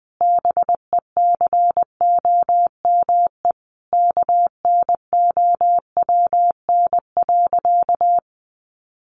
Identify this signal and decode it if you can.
Morse code, heard as a keyed tone, about 19 wpm, 700 Hz. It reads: BECOME KNOWN.
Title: Sentences from Top 1300 Words 20wpm